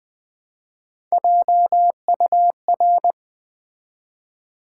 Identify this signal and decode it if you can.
Morse code, heard as a keyed tone, about 20 wpm, 700 Hz. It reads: JUR
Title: Build A Word - 1 Letter At A Time - 4-Letter Words - 20wpm